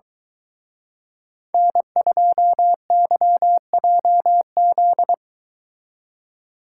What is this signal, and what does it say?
Morse code, heard as a keyed tone, about 23 wpm, 700 Hz. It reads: N2YJZ